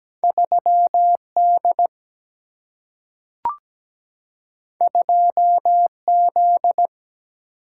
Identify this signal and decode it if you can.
Morse code, heard as a keyed tone, about 17 wpm, 700 Hz. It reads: 3D 2Z
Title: Single-Letter Single-Number 17wpm